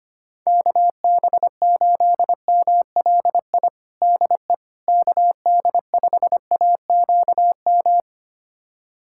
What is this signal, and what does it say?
Morse code, heard as a keyed tone, about 25 wpm, 700 Hz. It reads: KB8MLI DE KD5AQM
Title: USA Callsign DE Callsign 25wpm